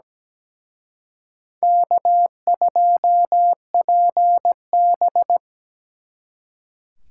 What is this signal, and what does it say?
Morse code, heard as a keyed tone, about 20 wpm, 700 Hz. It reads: K2PB
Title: Speed Racing - USA Call Signs 17wpm